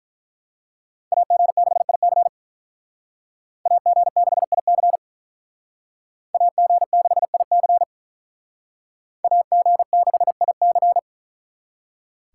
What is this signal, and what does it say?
Morse code, heard as a keyed tone, about 42 wpm, 700 Hz. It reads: AG6IC AG6IC AG6IC AG6IC
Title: Speed Racing - USA 2x2 Call Signs 35wpm